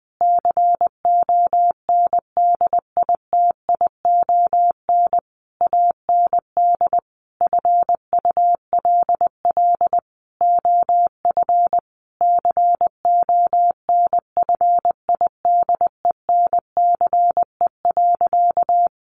Morse code, heard as a keyed tone, about 20 wpm, 700 Hz. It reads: CONDITION AND FULL OF CONFIDENCE.
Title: Sentences from Top 1500 Words 20wpm